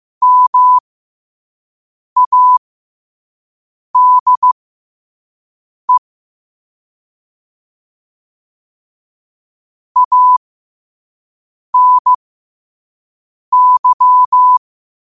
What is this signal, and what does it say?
Morse code, heard as a keyed tone, about 15 wpm, spaced as for 5 wpm, 1000 Hz. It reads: MADE ANY
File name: qotdfiles / 5wpm